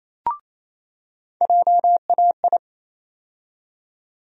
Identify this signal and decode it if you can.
Morse code, heard as a keyed tone, about 27 wpm, 700 Hz. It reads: JAI
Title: USA Call Sign Postfixes - 3 Letters 28wpm